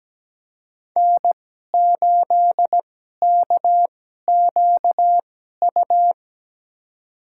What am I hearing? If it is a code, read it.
Morse code, heard as a keyed tone, about 17 wpm, spaced as for 12 wpm, 700 Hz. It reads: N8KQU
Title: USA Call Signs - 5 Characters - 2x Character Spacing - 17wpm